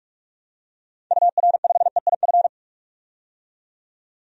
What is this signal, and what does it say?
Morse code, heard as a keyed tone, about 45 wpm, 700 Hz. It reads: UR5EIF
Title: DXCC-100 Call Signs - 6 Characters - 45wpm